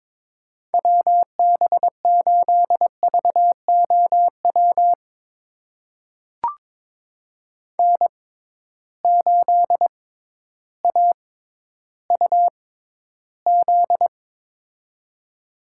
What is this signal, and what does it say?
Morse code, heard as a keyed tone, about 22 wpm, spaced as for 12 wpm, 700 Hz. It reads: WB8VOW N8AUZ
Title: USA Call Signs - 6x Character Spacing - 22wpm